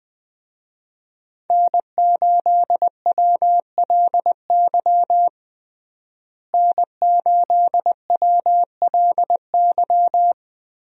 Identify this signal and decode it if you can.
Morse code, heard as a keyed tone, about 20 wpm, 700 Hz. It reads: N8WLY N8WLY
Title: USA Call Signs - 5 Characters - Repeated 3x - 20wpm